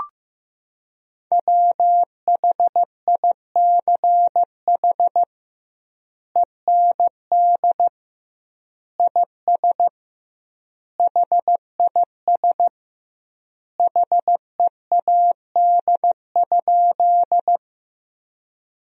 Morse code, heard as a keyed tone, about 15 wpm, 700 Hz. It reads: WHICH END IS HIS HEAD?
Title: 2x Word-Spacing - Sentences from Top 300 Words - 15wpm